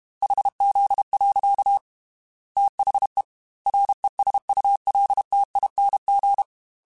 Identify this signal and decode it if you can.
Morse code, heard as a keyed tone, about 32 wpm, 800 Hz. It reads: HZ. THE RESULTING
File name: Technical Subjects / Technical Subjects-mp3s / XL @30